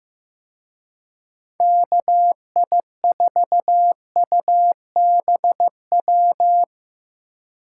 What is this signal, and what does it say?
Morse code, heard as a keyed tone, about 16 wpm, 700 Hz. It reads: KI4UBW
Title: Speed Racing - USA 2x3 Call Signs 15wpm